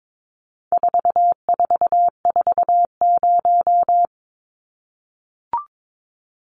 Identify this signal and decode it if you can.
Morse code, heard as a keyed tone, about 22 wpm, 700 Hz. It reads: 4440
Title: Numbers - 4-Digit 22wpm